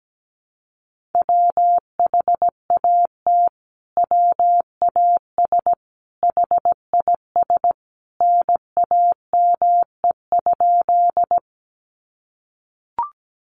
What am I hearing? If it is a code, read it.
Morse code, heard as a keyed tone, about 17 wpm, 700 Hz. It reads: WHAT WAS HIS NAME?
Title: Sentences from Top 200 Words 17wpm